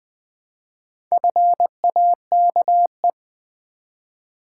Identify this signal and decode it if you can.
Morse code, heard as a keyed tone, about 20 wpm, 700 Hz. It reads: FAKE
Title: Build A Word - 1 Letter At A Time - 4-Letter Words - 20wpm